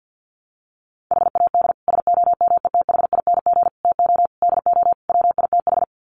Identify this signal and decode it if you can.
Morse code, heard as a keyed tone, about 51 wpm, 700 Hz. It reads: HAD SOMETHING TO DO WITH
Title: Sets of 5 Words 50wpm